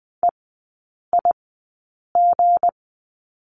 Morse code, heard as a keyed tone, about 20 wpm, spaced as for 8 wpm, 690 Hz. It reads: EIG
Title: 159 Lesson 26e - Letter g - Three Characters 20wpm